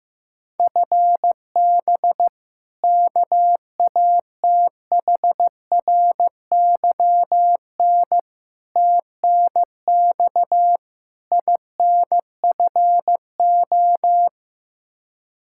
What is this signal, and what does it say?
Morse code, heard as a keyed tone, about 15 wpm, 700 Hz. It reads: FB KATHRYN TNX INFO
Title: Practice QSO - Exchange 3 - More Info - 15wpm